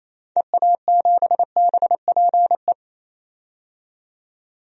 Morse code, heard as a keyed tone, about 28 wpm, 700 Hz. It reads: EA7BPE
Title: DXCC-100 Call Signs - 6 Characters - 28wpm